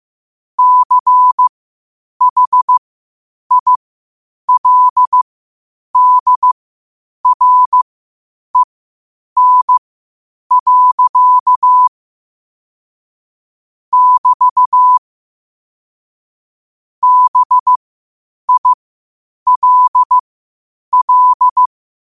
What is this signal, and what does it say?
Morse code, heard as a keyed tone, about 15 wpm, spaced as for 8 wpm, 1000 Hz. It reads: CHILDREN. <BT> BILL
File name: qotdfiles / 8wpm